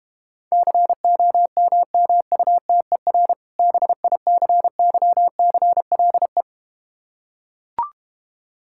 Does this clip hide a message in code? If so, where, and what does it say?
Yes: Morse code from 0.5 to 6.4 s, heard as a keyed tone, about 32 wpm, 700 Hz. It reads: COMMUTER BICYCLE